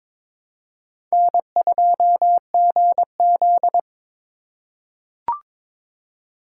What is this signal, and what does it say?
Morse code, heard as a keyed tone, about 22 wpm, 700 Hz. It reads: N2GZ
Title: USA Call Signs - 4 Characters - 4x Character Spacing - 22wpm